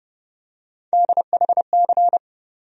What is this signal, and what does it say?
Morse code, heard as a keyed tone, about 30 wpm, 700 Hz. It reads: DHC
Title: USA Call Sign Postfixes - 3 Letters 30wpm